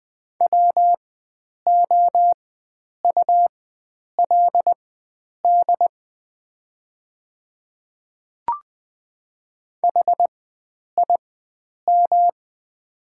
Morse code, heard as a keyed tone, about 20 wpm, spaced as for 9 wpm, 700 Hz. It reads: WOULD HIM
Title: 4x Character Spacing - Top 100 Words - 20wpm